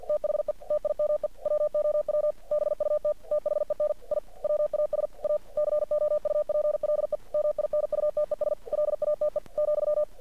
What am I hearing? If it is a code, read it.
Morse code, heard as a keyed tone, about 50 wpm, 600 Hz. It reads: THE TIME WOW BUT THEN I WAS A COUPLE MINUTES LATE ,
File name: archived / CRSnet